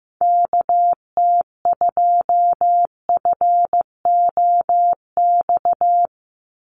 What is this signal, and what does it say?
Morse code, heard as a keyed tone, about 15 wpm, 700 Hz. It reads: KT2FOX
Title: USA Call Signs 15wpm